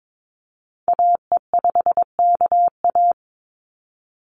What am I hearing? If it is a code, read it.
Morse code, heard as a keyed tone, about 22 wpm, 700 Hz. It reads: AE5KA
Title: NA SOTA Activator Callsigns 22wpm